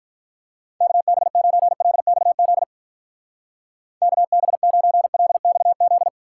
Morse code, heard as a keyed tone, about 47 wpm, 700 Hz. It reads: KB9LXZ KB9LXZ